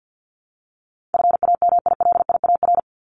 Morse code, heard as a keyed tone, about 48 wpm, 700 Hz. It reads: FAMILIAR
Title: Top 751-1000 Words 50wpm